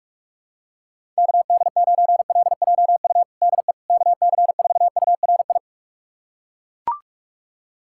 Morse code, heard as a keyed tone, about 41 wpm, 700 Hz. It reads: KD9LJU DE KC4URI
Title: USA Callsign DE Callsign 45wpm